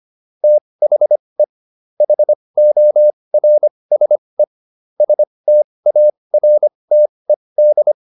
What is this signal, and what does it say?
Morse code, heard as a keyed tone, about 25 wpm, spaced as for 20 wpm, 600 Hz. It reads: THE HORSE STARTED
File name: SS123 20